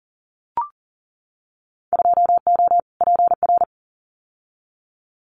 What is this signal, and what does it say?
Morse code, heard as a keyed tone, about 40 wpm, 700 Hz. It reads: 2O PR
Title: Winter Field Day Exchanges Encore 40wpm